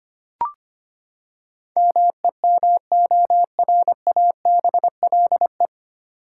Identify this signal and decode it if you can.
Morse code, heard as a keyed tone, about 25 wpm, 700 Hz. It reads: MEMORABLE
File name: Nine Letter Words - 25wpm